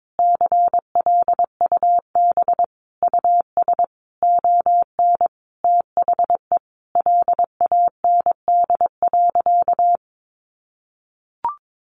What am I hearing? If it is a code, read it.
Morse code, heard as a keyed tone, about 22 wpm, 700 Hz. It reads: CLUB US ON THE LAND.
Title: Sentences from Top 1200 Words 22wpm